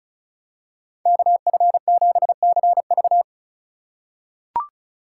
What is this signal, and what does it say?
Morse code, heard as a keyed tone, about 36 wpm, 700 Hz. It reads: KF7CV
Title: USA Call Signs - 4x Character Spacing - 35wpm